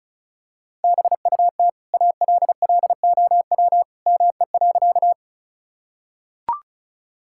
Morse code, heard as a keyed tone, about 34 wpm, 700 Hz. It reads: BUT ALLOW ME.
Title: Sentences from Top 1000 Words - Review Part 5 - 35wpm